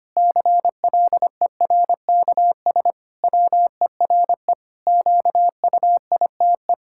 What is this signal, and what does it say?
Morse code, heard as a keyed tone, about 25 wpm, 700 Hz. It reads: CLERKS WERE QUITE
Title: Pangram Challenge v2 - 25wpm